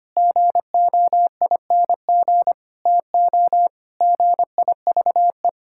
Morse code, heard as a keyed tone, about 25 wpm, 700 Hz. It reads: GOING TO GIVE
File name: practice / 25wpm / Sentences with 4-Word Phrases - 25wpm